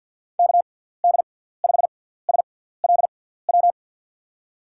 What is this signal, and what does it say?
Morse code, heard as a keyed tone, about 47 wpm, spaced as for 17 wpm, 700 Hz. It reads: KD5SLW
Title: USA Call Signs - 6 Characters - 6x Character Spacing - 50wpm